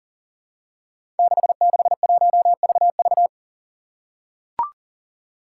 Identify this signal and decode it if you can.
Morse code, heard as a keyed tone, about 40 wpm, 700 Hz. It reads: 661VV
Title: Five Challenging Letter-Number Combinations 40wpm